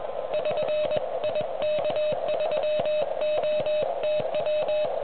Morse code, heard as a keyed tone, about 20 wpm, 610 Hz, with noise. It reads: <SN> IK3OTW